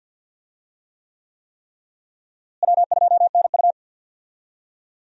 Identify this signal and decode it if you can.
Morse code, heard as a keyed tone, about 48 wpm, 700 Hz. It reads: W2NU
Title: USA Call Signs - 4 Characters - Repeated 3x - 50wpm